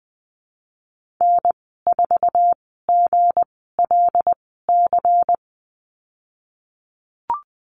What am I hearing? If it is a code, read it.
Morse code, heard as a keyed tone, about 19 wpm, spaced as for 14 wpm, 700 Hz. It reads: N4GLC
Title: 173 Lesson 28g - Number 9 - Call Signs 20wpm